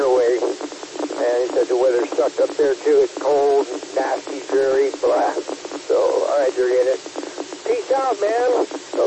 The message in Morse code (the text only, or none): none